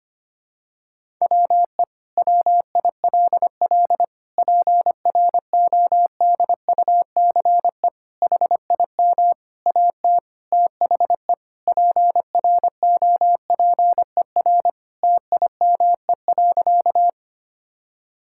Morse code, heard as a keyed tone, about 25 wpm, 700 Hz. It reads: WE WILL PRODUCE HIM AT THE PROPER TIME.
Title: Sentences from Top 1000 Words - Review Part 5 - 25wpm